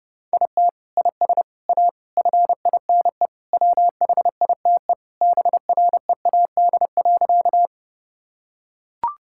Morse code, heard as a keyed tone, about 32 wpm, 700 Hz. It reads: IT IS A FINE WHITE BREAD.
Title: Sentences from Top 1600 Words 30wpm